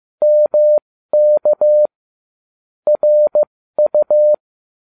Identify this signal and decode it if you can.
Morse code, heard as a keyed tone, about 15 wpm, spaced as for 13 wpm, 600 Hz. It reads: MK RU